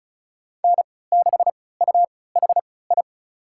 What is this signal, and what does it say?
Morse code, heard as a keyed tone, about 36 wpm, spaced as for 20 wpm, 700 Hz. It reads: N6UHI
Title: USA Call Signs - 5 Characters - 3x Character Spacing - 35wpm